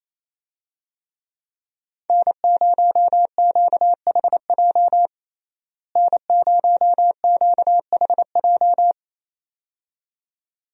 Morse code, heard as a keyed tone, about 28 wpm, 700 Hz. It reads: N0QHJ N0QHJ
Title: USA Call Signs - 5 Characters - Repeated 2x - 28wpm